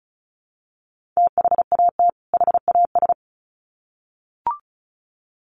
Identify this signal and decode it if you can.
Morse code, heard as a keyed tone, about 34 wpm, 700 Hz. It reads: THAT HAS
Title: Sets of 2 Words 35wpm